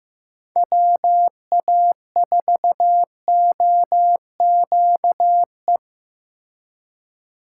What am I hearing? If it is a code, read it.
Morse code, heard as a keyed tone, about 18 wpm, 700 Hz. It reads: WA4OQE
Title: Speed Racing - USA Call Signs 15wpm